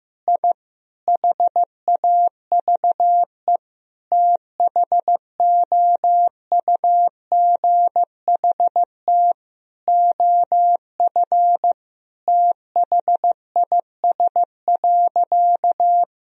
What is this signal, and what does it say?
Morse code, heard as a keyed tone, about 15 wpm, 700 Hz. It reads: I HAVE THOUGHT OF THIS.
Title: Sentences from Top 1000 Words - Review Part 1 - 15wpm